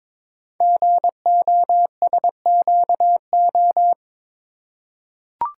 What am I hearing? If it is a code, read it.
Morse code, heard as a keyed tone, about 22 wpm, 690 Hz. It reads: GOSQO
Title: Code Groups - 5 Letters - 22wpm